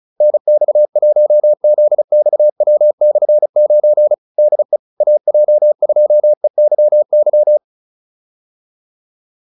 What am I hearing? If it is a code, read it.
Morse code, heard as a keyed tone, about 34 wpm, 600 Hz. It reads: NX1ZXW/9 DE AJ2EYY